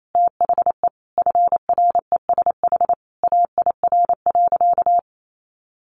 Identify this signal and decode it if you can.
Morse code, heard as a keyed tone, about 28 wpm, 700 Hz. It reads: THE FRESH AIR.
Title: Sentences from Top 1200 Words 28wpm